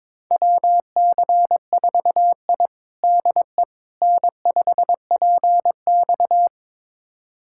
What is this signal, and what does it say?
Morse code, heard as a keyed tone, about 22 wpm, 700 Hz. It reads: WC4I DE N5PX
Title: USA Callsign DE Callsign - 4 Characters - 22wpm